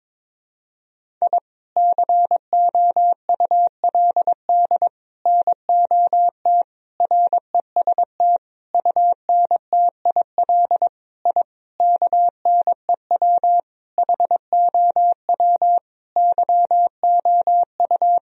Morse code, heard as a keyed tone, about 22 wpm, 700 Hz. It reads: I COULD NOT REST UNTIL I KNEW HOW YOU
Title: Sentences from Top 1000 Words - Review Part 1 - 22wpm